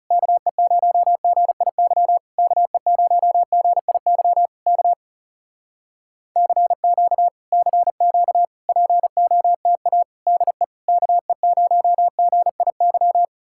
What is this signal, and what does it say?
Morse code, heard as a keyed tone, about 40 wpm, 700 Hz. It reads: KE0GIY KE0GIY K CQ CQ POTA DE KE0GIY